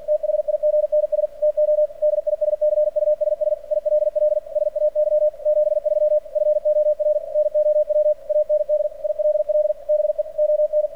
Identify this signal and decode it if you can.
Morse code, heard as a keyed tone, about 47 wpm, 600 Hz. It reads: THIGNS TO DISCUSS ILL SAY 73 FOR NOW AND ILL BE ON
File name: archived / CRSnet